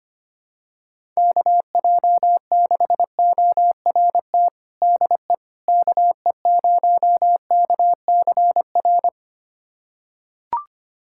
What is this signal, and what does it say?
Morse code, heard as a keyed tone, about 25 wpm, 700 Hz. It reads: KJ6ORT DE KE0KCR
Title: USA Callsign DE Callsign - 6 Characters - 25wpm